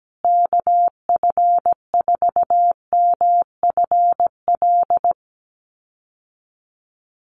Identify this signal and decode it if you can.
Morse code, heard as a keyed tone, about 17 wpm, 700 Hz. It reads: KF4MFL
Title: USA Call Signs Repeated 2x 17wpm